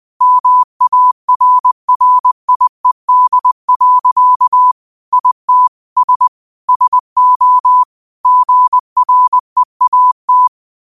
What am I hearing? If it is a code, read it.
Morse code, heard as a keyed tone, about 20 wpm, 1000 Hz. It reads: MARRIED. IT S SO GREAT
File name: qotdfiles / 20wpm